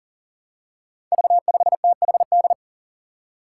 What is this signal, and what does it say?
Morse code, heard as a keyed tone, about 42 wpm, 700 Hz. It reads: V5THD
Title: Five Challenging Letter-Number Combinations 40wpm